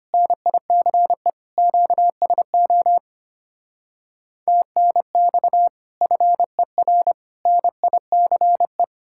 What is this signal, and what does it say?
Morse code, heard as a keyed tone, about 30 wpm, 700 Hz. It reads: NICE QSO TNX FER NICE
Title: Speed-Racing Common QSO Phrases - 25wpm